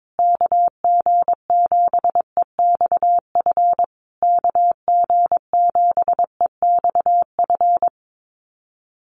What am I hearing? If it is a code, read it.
Morse code, heard as a keyed tone, about 22 wpm, 700 Hz. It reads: KG7EXF KG7EXF
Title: USA Call Signs Repeated 3x 22wpm